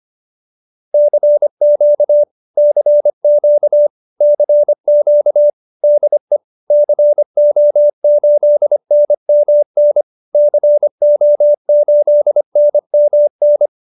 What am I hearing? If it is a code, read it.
Morse code, heard as a keyed tone, about 25 wpm, 590 Hz. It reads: CQ CQ CQ DE CO8NMN CO8NMN
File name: QSO304 25